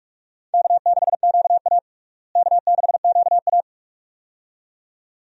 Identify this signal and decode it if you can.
Morse code, heard as a keyed tone, about 47 wpm, 700 Hz. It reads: K6QA K6QA
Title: USA Call Signs - 4 Characters - Repeated 2x - 45wpm